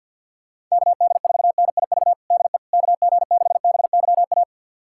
Morse code, heard as a keyed tone, about 50 wpm, 700 Hz. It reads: KD4NIV DE KG6BYA
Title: USA Callsign DE Callsign 50wpm